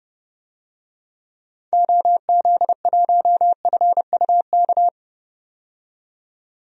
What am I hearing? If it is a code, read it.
Morse code, heard as a keyed tone, about 30 wpm, 700 Hz. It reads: OZ1FUK
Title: DXCC-100 Call Signs - 6 Characters - 30wpm